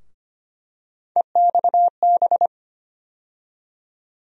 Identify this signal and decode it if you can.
Morse code, heard as a keyed tone, about 25 wpm, 700 Hz. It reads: EXB
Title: USA Call Sign Postfixes - 3 Letters 25wpm